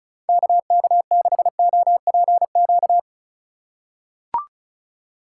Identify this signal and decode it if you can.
Morse code, heard as a keyed tone, about 36 wpm, 700 Hz. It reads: KK6OPQ
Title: USA Call Signs - 6 Characters - 3x Character Spacing - 35wpm